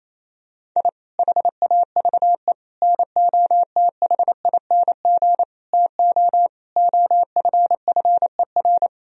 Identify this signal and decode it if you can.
Morse code, heard as a keyed tone, about 28 wpm, 700 Hz. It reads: I HAVE NOTHING TO OFFER
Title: Sentences from Top 1000 Words - Review Part 4 - 28wpm